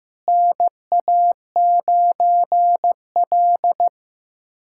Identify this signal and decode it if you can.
Morse code, heard as a keyed tone, about 15 wpm, 700 Hz. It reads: NA9L
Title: USA Callsign DE Callsign - 4 Characters - 15wpm